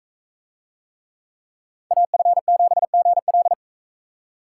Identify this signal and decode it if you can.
Morse code, heard as a keyed tone, about 42 wpm, 700 Hz. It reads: AF7GL